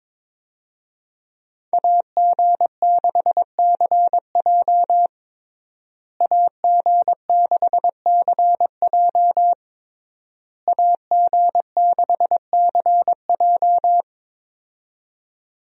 Morse code, heard as a keyed tone, about 21 wpm, 700 Hz. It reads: AG6CJ AG6CJ AG6CJ
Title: USA Call Signs - 5 Characters - Repeated 3x - 22wpm